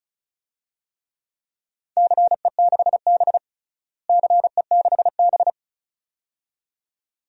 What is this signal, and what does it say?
Morse code, heard as a keyed tone, about 36 wpm, 700 Hz. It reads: CE6B CE6B
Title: DXCC-100 Call Signs - 4 Characters - Repeated 2x - 35wpm